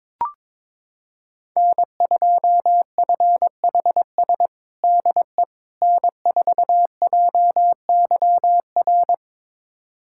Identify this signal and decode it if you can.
Morse code, heard as a keyed tone, about 22 wpm, 700 Hz. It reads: N2FHS DE N4JYR
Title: USA Callsign DE Callsign - 5 Characters - 22wpm